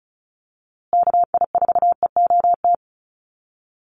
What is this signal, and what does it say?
Morse code, heard as a keyed tone, about 34 wpm, 700 Hz. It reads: KI4EOT
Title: USA Call Signs Repeated 3x 35wpm